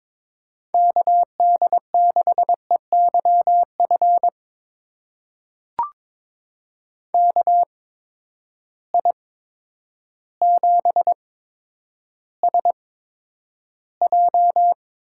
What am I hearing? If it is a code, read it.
Morse code, heard as a keyed tone, about 22 wpm, spaced as for 10 wpm, 700 Hz. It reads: KD6EYF KI7SJ
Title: USA Call Signs - 6 Characters - 8x Character Spacing - 22wpm